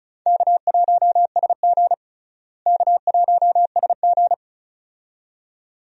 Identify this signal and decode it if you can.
Morse code, heard as a keyed tone, about 36 wpm, 700 Hz. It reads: K1SG K1SG
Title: USA Call Signs - 4 Characters - Repeated 2x - 35wpm